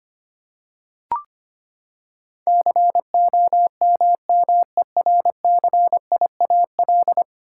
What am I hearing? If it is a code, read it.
Morse code, heard as a keyed tone, about 25 wpm, 700 Hz. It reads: COMMERCIAL
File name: Ten Letter Words - 25wpm